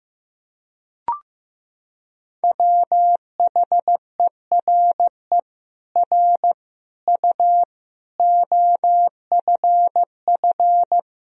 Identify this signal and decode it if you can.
Morse code, heard as a keyed tone, about 15 wpm, 700 Hz. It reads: WHERE R U OFF
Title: Sentences from Top 400 Words and Abbreviations 15wpm